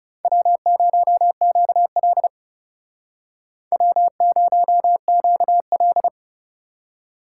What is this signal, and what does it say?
Morse code, heard as a keyed tone, about 36 wpm, 700 Hz. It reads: W0QL W0QL
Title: Speed Racing - USA 1x2 Call Signs 30wpm